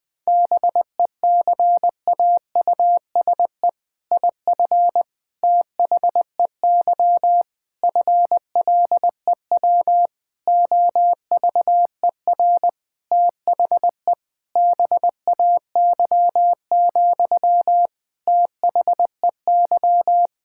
Morse code, heard as a keyed tone, about 20 wpm, 700 Hz. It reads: BECAUSE IF THEY FLEW OVER THE BAY, THEY
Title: All-Age Appropriate Jokes - 20wpm